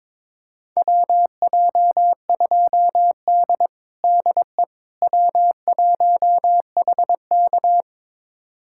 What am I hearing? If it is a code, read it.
Morse code, heard as a keyed tone, about 22 wpm, 700 Hz. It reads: WJ2D DE W1HK
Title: USA Callsign DE Callsign - 4 Characters - 22wpm